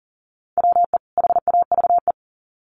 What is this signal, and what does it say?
Morse code, heard as a keyed tone, about 40 wpm, 700 Hz. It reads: WE HAVE
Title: Sets of 2 Words 40wpm